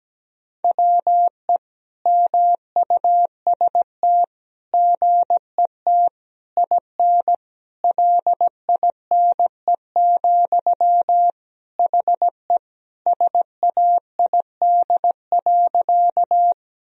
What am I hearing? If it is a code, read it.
Morse code, heard as a keyed tone, about 17 wpm, 700 Hz. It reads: WE MUST GET IN LINE, HE SAID.
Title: Sentences from Top 1000 Words - Review Part 1 - 17wpm